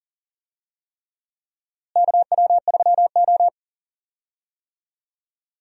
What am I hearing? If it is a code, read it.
Morse code, heard as a keyed tone, about 40 wpm, 700 Hz. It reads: KW3O
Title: USA Call Signs - 4 Characters - Repeated 3x - 40wpm